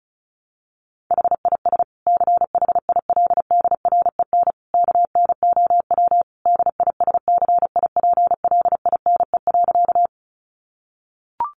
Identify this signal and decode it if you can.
Morse code, heard as a keyed tone, about 34 wpm, 700 Hz. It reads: HIS CHILDREN KNOW DISCIPLINE.
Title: Sentences from Top 900 Words 35wpm